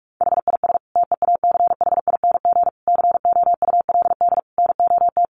Morse code, heard as a keyed tone, about 44 wpm, 700 Hz. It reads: HIS TEACHING COULD NOT
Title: Sentences from Top 1100 Words 45wpm